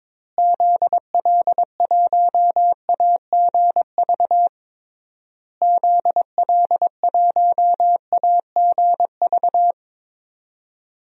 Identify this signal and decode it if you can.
Morse code, heard as a keyed tone, about 22 wpm, 700 Hz. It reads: ZL1AGV ZL1AGV
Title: DXCC-100 Call Signs - 6 Characters - Repeated 3x - 22wpm